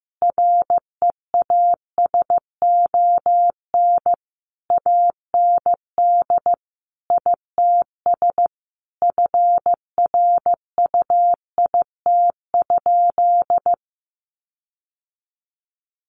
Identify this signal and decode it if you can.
Morse code, heard as a keyed tone, about 15 wpm, 700 Hz. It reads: REASON AND ITS FRUIT?
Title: Sentences from Top 1500 Words 15wpm